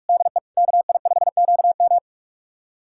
Morse code, heard as a keyed tone, about 46 wpm, 700 Hz. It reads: DE KI5QM
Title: USA Callsign DE Callsign 45wpm